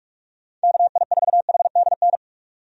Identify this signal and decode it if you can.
Morse code, heard as a keyed tone, about 42 wpm, 700 Hz. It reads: KI4HDN
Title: USA Call Signs - 6 Characters - 45wpm